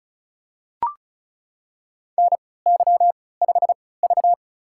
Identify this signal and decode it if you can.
Morse code, heard as a keyed tone, about 36 wpm, spaced as for 20 wpm, 700 Hz. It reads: NY5V
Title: USA Call Signs - 4 Characters - 3x Character Spacing - 35wpm